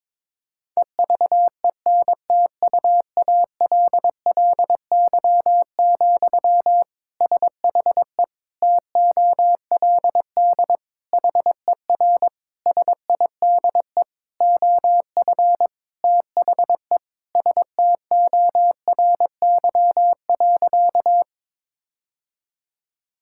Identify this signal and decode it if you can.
Morse code, heard as a keyed tone, about 22 wpm, 700 Hz. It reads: EVENTUALLY, SHE TOLD HER SIDE OF THE STORY.
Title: Sentences from Top 1000 Words - Review Part 6 - 22wpm